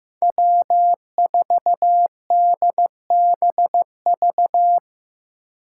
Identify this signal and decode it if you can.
Morse code, heard as a keyed tone, about 18 wpm, 700 Hz. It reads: W4DBV
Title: Speed Racing - USA 1x3 Call Signs 15wpm